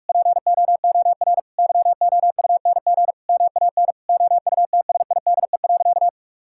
Morse code, heard as a keyed tone, about 41 wpm, 700 Hz. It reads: POOR YOUNG MAN OUTSIDE.
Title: Sentences from Top 1000 Words - Review Part 6 - 45wpm